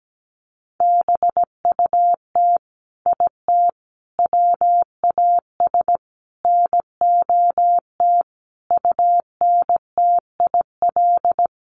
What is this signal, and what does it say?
Morse code, heard as a keyed tone, about 17 wpm, 700 Hz. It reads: BUT IT WAS NOT UNTIL
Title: Sets of 5 Words Encore2 17wpm